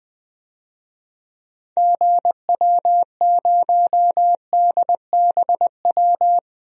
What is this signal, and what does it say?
Morse code, heard as a keyed tone, about 20 wpm, 700 Hz. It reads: GW0DBW
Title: DXCC-100 Call Signs - 6 Characters - 20wpm